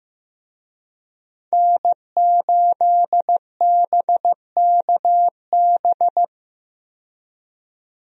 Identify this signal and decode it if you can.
Morse code, heard as a keyed tone, about 15 wpm, 700 Hz. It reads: N8BKB
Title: USA Call Signs - 5 Characters - 15wpm